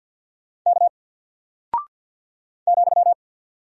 Morse code, heard as a keyed tone, about 48 wpm, 700 Hz. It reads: K ,
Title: Single Character (Letter:Number:Punctuation:Prosign) 50wpm